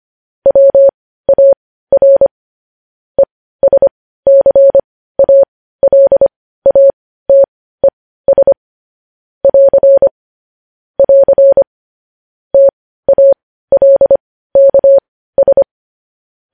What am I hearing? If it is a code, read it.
Morse code, heard as a keyed tone, about 25 wpm, spaced as for 15 wpm, 550 Hz. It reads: WAR ESCALATES <AR> <AR> TALKS
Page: news-headlines-in-morse-code-at-15-wpm